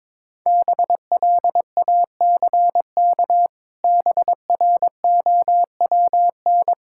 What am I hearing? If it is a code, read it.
Morse code, heard as a keyed tone, about 22 wpm, 700 Hz. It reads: BLACK BROWN
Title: Pangram Challenge v2 - 22wpm